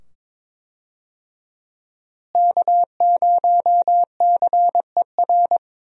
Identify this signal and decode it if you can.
Morse code, heard as a keyed tone, about 22 wpm, 700 Hz. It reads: K0CER
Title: USA Call Signs - 5 Characters - Repeated 2x - 22wpm